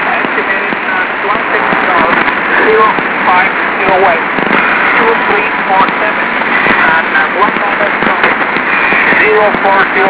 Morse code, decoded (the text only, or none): none